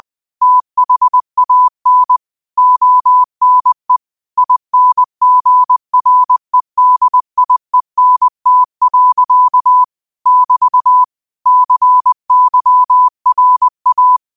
Morse code, heard as a keyed tone, about 20 wpm, 1000 Hz. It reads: THAN ONE INGREDIENT. <BT> CYRA
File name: 20wpm